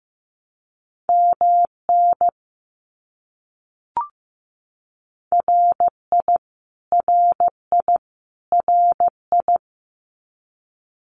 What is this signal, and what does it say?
Morse code, heard as a keyed tone, about 15 wpm, 700 Hz. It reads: MN RI RI RI
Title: US States Repeated 3x 15wpm